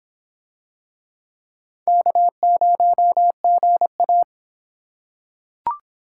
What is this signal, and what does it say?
Morse code, heard as a keyed tone, about 25 wpm, 700 Hz. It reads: K0GA